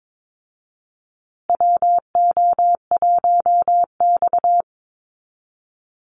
Morse code, heard as a keyed tone, about 22 wpm, 690 Hz. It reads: WO1X
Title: Call Signs Repeated 2x - 22wpm